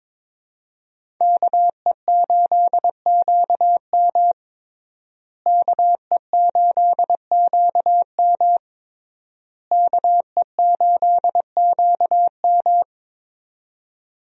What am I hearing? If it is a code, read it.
Morse code, heard as a keyed tone, about 22 wpm, 700 Hz. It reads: KE8QM KE8QM KE8QM
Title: USA Call Signs - 5 Characters - Repeated 3x - 22wpm